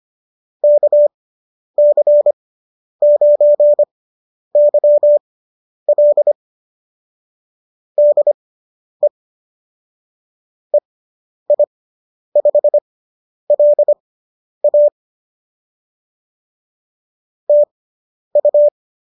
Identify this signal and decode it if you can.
Morse code, heard as a keyed tone, about 25 wpm, spaced as for 10 wpm, 590 Hz. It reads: KC9YL DE EI5LA TU